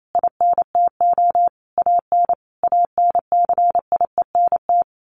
Morse code, heard as a keyed tone, about 28 wpm, 700 Hz. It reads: INTO AN ANCIENT
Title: A Sound Of Thunder 28wpm